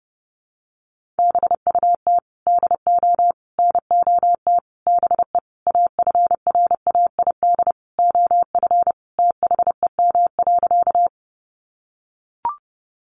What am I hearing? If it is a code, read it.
Morse code, heard as a keyed tone, about 30 wpm, 700 Hz. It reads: BUT DO NOT BE AFRAID OF THEM.
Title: Sentences from Top 1500 Words 30wpm